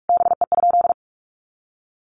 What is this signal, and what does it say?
Morse code, heard as a keyed tone, about 44 wpm, 700 Hz. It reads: BE?
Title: Sentences from Top 1500 Words 45wpm